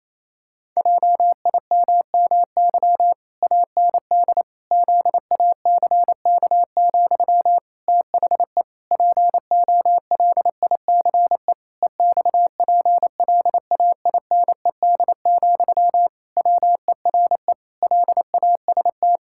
Morse code, heard as a keyed tone, about 28 wpm, 700 Hz. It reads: JIMMY AND ZACK, THE POLICE EXPLAINED, WERE LAST
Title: Pangram Challenge v2 - 28wpm